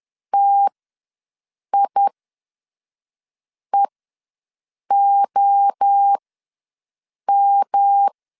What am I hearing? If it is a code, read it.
Morse code, heard as a keyed tone, about 10 wpm, spaced as for 5 wpm, 800 Hz. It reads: TIEOM
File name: LEZIONI / 7 GF9Z / 7 GF9Z ws 10wpm 3919